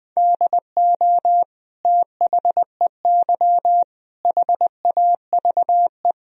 Morse code, heard as a keyed tone, about 20 wpm, 700 Hz. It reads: DO THEY HAVE
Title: Sentences from Top 1000 Words - Review Part 1 - 20wpm